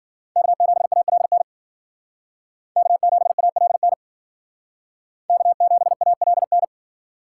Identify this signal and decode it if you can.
Morse code, heard as a keyed tone, about 53 wpm, 700 Hz. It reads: K7ALN K7ALN K7ALN